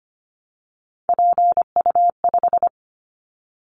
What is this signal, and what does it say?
Morse code, heard as a keyed tone, about 25 wpm, 700 Hz. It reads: PU5
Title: Callsign Prefixes for Top 50 DXCC - 25wpm